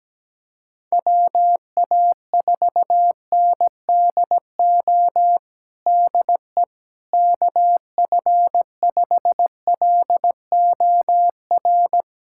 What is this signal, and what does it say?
Morse code, heard as a keyed tone, about 17 wpm, 700 Hz. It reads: WA4NDO DE KF5LOR
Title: USA Callsign DE Callsign 17wpm